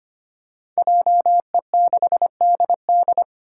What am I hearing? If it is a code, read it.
Morse code, heard as a keyed tone, about 25 wpm, 700 Hz. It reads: JE6DD
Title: Five Challenging Letter-Number Combinations 25wpm